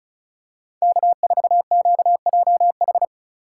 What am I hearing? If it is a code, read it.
Morse code, heard as a keyed tone, about 36 wpm, 700 Hz. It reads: K4QJH
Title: USA Call Signs - 5 Characters - 2x Character Spacing - 35wpm